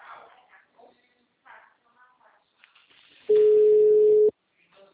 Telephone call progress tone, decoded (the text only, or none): ringback tone